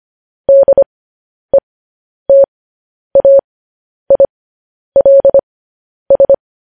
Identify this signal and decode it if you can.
Morse code, heard as a keyed tone, about 25 wpm, spaced as for 10 wpm, 550 Hz. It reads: DETAILS